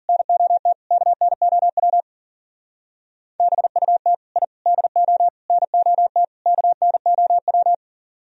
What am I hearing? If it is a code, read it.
Morse code, heard as a keyed tone, about 50 wpm, 700 Hz. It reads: NOT KNOW BUT I DO NOT KNOW
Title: Speed-Racing - Sets of 5 Words 40wpm